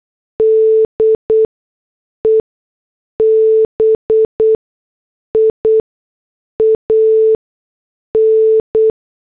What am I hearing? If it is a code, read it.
Morse code, heard as a keyed tone, about 8 wpm, spaced as for 6 wpm, 440 Hz. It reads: DEBIAN
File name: morse / headlines